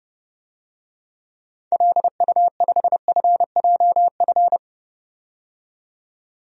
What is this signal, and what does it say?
Morse code, heard as a keyed tone, about 30 wpm, 700 Hz. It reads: LU5FJF
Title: DXCC-100 Call Signs - 6 Characters - 30wpm